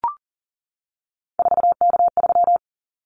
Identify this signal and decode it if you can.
Morse code, heard as a keyed tone, about 40 wpm, 700 Hz. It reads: 4K3
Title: Callsign Prefixes for Top 100 DXCC - 40wpm